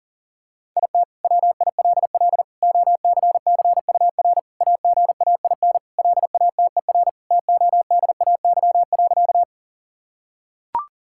Morse code, heard as a keyed tone, about 40 wpm, 700 Hz. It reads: IT WILL OCCUR AGAIN LATER TODAY.
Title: Sentences from Top 1000 Words - Review Part 6 - 40wpm